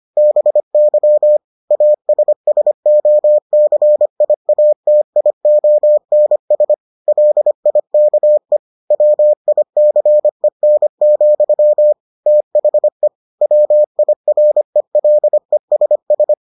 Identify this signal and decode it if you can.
Morse code, heard as a keyed tone, about 25 wpm, 600 Hz. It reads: BY ASSOCIATIONS LIKE WICEN, THE WIRELESS